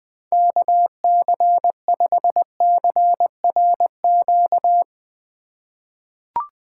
Morse code, heard as a keyed tone, about 20 wpm, 700 Hz. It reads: KC5CRQ
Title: USA Call Signs - 6 Characters - 2x Character Spacing - 20wpm